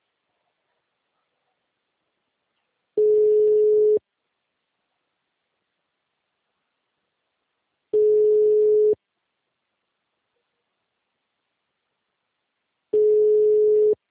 Ringback tone (Europe, 425 Hz).